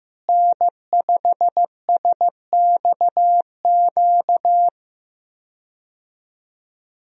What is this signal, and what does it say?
Morse code, heard as a keyed tone, about 15 wpm, 700 Hz. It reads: N5SXQ
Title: USA Call Signs - 5 Characters - Repeated 2x - 15wpm